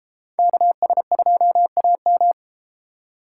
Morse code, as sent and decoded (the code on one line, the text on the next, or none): -.- ... ..--- .- --
KS2AM